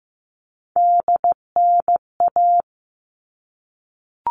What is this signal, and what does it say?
Morse code, heard as a keyed tone, about 15 wpm, 700 Hz. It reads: DNA
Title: All Cut Numbers - 3-Digit 15wpm